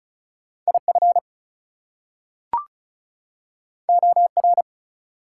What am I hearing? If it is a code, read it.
Morse code, heard as a keyed tone, about 36 wpm, 700 Hz. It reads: IF OR
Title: Two Letter Words - 35wpm